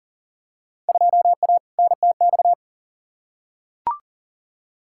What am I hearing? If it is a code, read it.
Morse code, heard as a keyed tone, about 40 wpm, 700 Hz. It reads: 2A NTX
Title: Field Day Exchanges - 40wpm